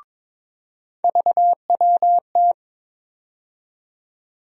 Morse code, heard as a keyed tone, about 22 wpm, 700 Hz. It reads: VWT
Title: USA Call Sign Postfixes - 3 Letters 22wpm